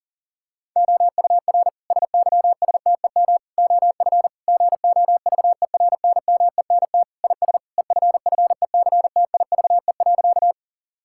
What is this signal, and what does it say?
Morse code, heard as a keyed tone, about 40 wpm, 700 Hz. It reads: OUR SYSTEM OF GOVERNMENT IS EFFECTIVE.
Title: Sentences from Top 1000 Words - Review Part 6 - 40wpm